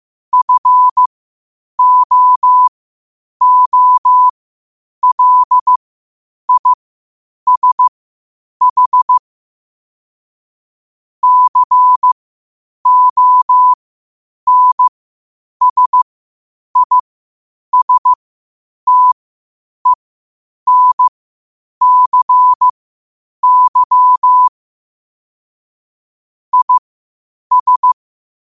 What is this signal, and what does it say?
Morse code, heard as a keyed tone, about 15 wpm, spaced as for 8 wpm, 1000 Hz. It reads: FOOLISH CONSISTENCY IS
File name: qotdfiles / 8wpm